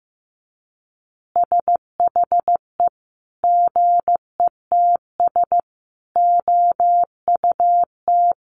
Morse code, heard as a keyed tone, about 15 wpm, 700 Hz. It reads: SHE GETS OUT
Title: Easy Phrases 15wpm